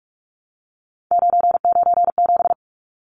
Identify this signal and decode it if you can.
Morse code, heard as a keyed tone, about 44 wpm, 700 Hz. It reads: 997
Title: Numbers - 3-Digit 45wpm